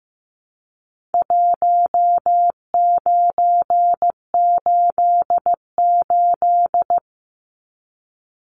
Morse code, heard as a keyed tone, about 15 wpm, 700 Hz. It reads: 1988
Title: Numbers - Contest Style - 4-Digit 15wpm